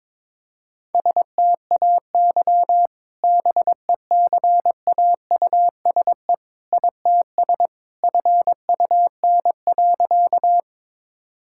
Morse code, heard as a keyed tone, about 22 wpm, 700 Hz. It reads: STAY BECAUSE ITS FUN.
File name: practice / 22wpm / Sentences with 2-Word Phrases - 22wpm